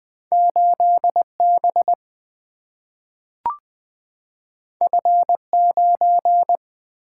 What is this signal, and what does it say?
Morse code, heard as a keyed tone, about 20 wpm, 700 Hz. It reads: 8 B F 9
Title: Single-Letter Single-Number 20wpm